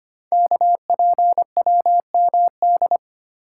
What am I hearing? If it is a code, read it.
Morse code, heard as a keyed tone, about 25 wpm, 700 Hz. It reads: KPWMD
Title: Code Groups - 5 Letters - 25wpm